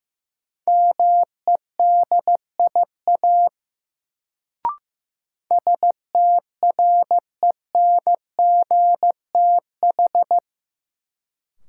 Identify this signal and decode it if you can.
Morse code, heard as a keyed tone, about 16 wpm, 700 Hz. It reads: MEDIA STRENGTH
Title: Sentences from Top 1000 Words - Review Part 4 - New Word Drill - 15wpm